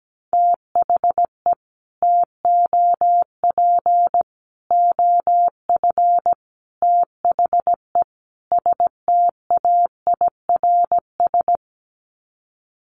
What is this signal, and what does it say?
Morse code, heard as a keyed tone, about 17 wpm, 700 Hz. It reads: THE TOP OF THE STAIRS
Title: Sets of 5 Words Encore 17wpm